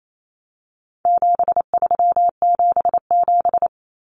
Morse code, heard as a keyed tone, about 28 wpm, 700 Hz. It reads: 7377
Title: Common Cut Numbers - 4-Digit 28wpm